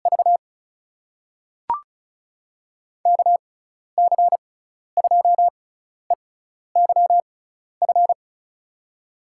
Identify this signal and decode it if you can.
Morse code, heard as a keyed tone, about 36 wpm, spaced as for 12 wpm, 700 Hz. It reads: V KC2EYF
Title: USA Call Signs - 6 Characters - 6x Character Spacing - 35wpm